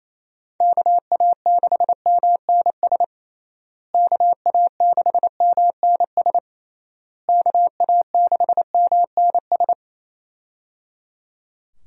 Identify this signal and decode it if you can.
Morse code, heard as a keyed tone, about 27 wpm, 700 Hz. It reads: KA6MNS KA6MNS KA6MNS
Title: USA Call Signs - 6 Characters - Repeated 3x - 28wpm